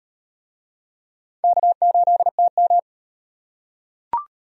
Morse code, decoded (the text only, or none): K8TM